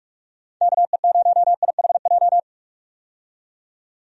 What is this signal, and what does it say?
Morse code, heard as a keyed tone, about 46 wpm, 700 Hz. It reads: KE0IHJ
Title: USA Call Signs - 6 Characters - 45wpm